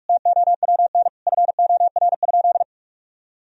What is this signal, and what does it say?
Morse code, heard as a keyed tone, about 46 wpm, 700 Hz. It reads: TOWN FOR?
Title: Sentences from Top 1000 Words - Review Part 2 - 45wpm